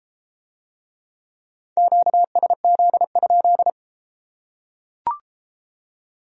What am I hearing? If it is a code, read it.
Morse code, heard as a keyed tone, about 32 wpm, 700 Hz. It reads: QSZ?